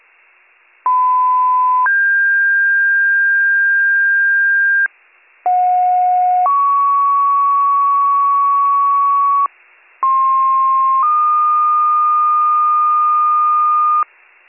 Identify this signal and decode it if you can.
Two-tone page, heard as a one-second tone, then a three-second tone, about 1010 then 1600 Hz, about 730 then 1120 Hz, about 1060 then 1220 Hz. It is heard from 0.8 to 14.0 s.